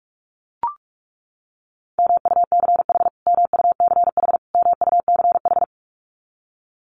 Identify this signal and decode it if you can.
Morse code, heard as a keyed tone, about 44 wpm, 700 Hz. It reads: MUCH MUCH MUCH
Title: Top 500 Words Repeated 45wpm